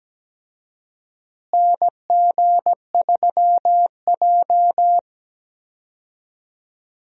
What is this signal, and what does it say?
Morse code, heard as a keyed tone, about 17 wpm, 700 Hz. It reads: NG3J
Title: USA Call Signs - 4 Characters - Repeated 2x - 17wpm